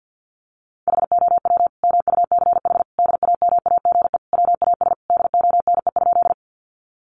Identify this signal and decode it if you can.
Morse code, heard as a keyed tone, about 50 wpm, 700 Hz. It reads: HOW MUCH DAMAGE WAS DONE?
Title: Sentences from Top 1100 Words 50wpm